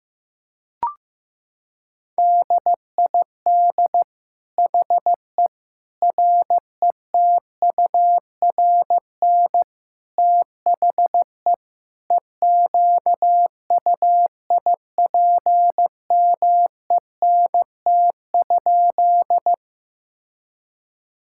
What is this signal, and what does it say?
Morse code, heard as a keyed tone, about 15 wpm, 700 Hz. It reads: DID HE RETURN THE EQUIPMENT?
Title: Sentences from Top 1000 Words - Review Part 5 - 15wpm